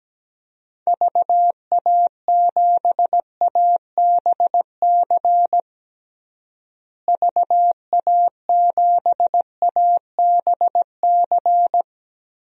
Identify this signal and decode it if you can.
Morse code, heard as a keyed tone, about 17 wpm, 700 Hz. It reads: VA7ABC VA7ABC
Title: DXCC-100 Call Signs - 6 Characters - Repeated 2x - 17wpm